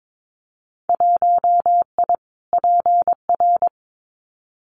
Morse code, heard as a keyed tone, about 22 wpm, 700 Hz. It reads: 1I PR
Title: Winter Field Day Exchanges Encore 22wpm